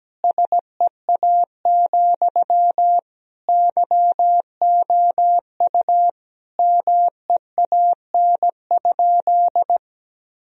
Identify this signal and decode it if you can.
Morse code, heard as a keyed tone, about 17 wpm, 700 Hz. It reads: SEA, YOU MEAN?
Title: Sentences from Top 1000 Words - Review Part 1 - 17wpm